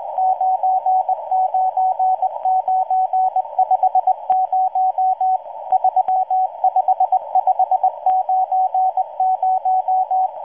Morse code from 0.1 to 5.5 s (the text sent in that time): 99950